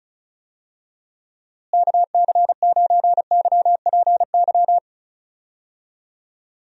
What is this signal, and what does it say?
Morse code, heard as a keyed tone, about 36 wpm, 700 Hz. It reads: KC9YPY
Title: USA Call Signs - 6 Characters - Repeated 2x - 35wpm